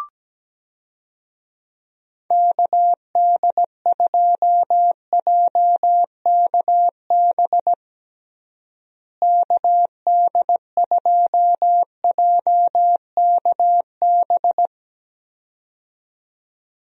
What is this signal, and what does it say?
Morse code, heard as a keyed tone, about 17 wpm, 700 Hz. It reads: KD2JKB KD2JKB
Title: USA Call Signs - 6 Characters - Repeated 2x - 17wpm